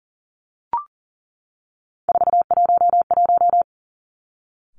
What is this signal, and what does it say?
Morse code, heard as a keyed tone, about 38 wpm, 700 Hz. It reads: 411
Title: Three Challenging Letter-Number Combinations 40wpm